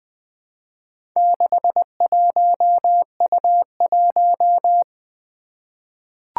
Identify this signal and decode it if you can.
Morse code, heard as a keyed tone, about 20 wpm, 700 Hz. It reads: 61U1
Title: Four Challenging Letter-Number Combinations 20wpm